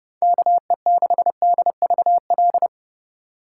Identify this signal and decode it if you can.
Morse code, heard as a keyed tone, about 28 wpm, 700 Hz. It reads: KE6DVL